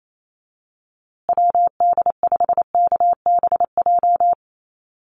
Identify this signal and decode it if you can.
Morse code, heard as a keyed tone, about 28 wpm, 700 Hz. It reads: WD5KBJ
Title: USA Call Signs Repeated 2x 28wpm